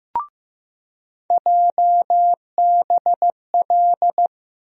Morse code, heard as a keyed tone, about 15 wpm, 700 Hz. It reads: JBL
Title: USA Call Sign Postfixes - 3 Letters 15wpm